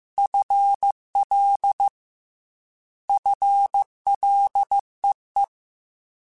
Morse code, heard as a keyed tone, about 15 wpm, 800 Hz. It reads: FL FLEE
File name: Combinations / Combinations-mp3s / Combinations@13